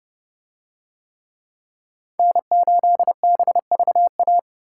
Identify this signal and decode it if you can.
Morse code, heard as a keyed tone, about 29 wpm, 700 Hz. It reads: N8BVA